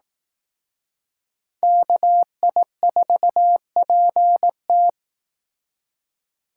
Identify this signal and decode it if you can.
Morse code, heard as a keyed tone, about 18 wpm, 700 Hz. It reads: KI4PT
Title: Speed Racing - USA 2x2 Call Signs 15wpm